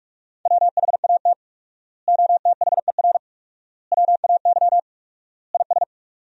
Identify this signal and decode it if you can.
Morse code, heard as a keyed tone, about 46 wpm, 700 Hz. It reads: WHAT OTHER WAY IS
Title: 4x Word-Spacing - Sentences from Top 100 Words - 45wpm